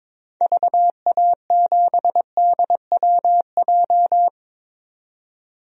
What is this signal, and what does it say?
Morse code, heard as a keyed tone, about 22 wpm, 700 Hz. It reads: VA7DWJ
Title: DXCC-100 Call Signs - 6 Characters - 22wpm